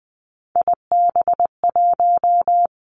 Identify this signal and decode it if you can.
Morse code, heard as a keyed tone, about 20 wpm, 700 Hz. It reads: IB1
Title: Three Challenging Letter-Number Combinations 20wpm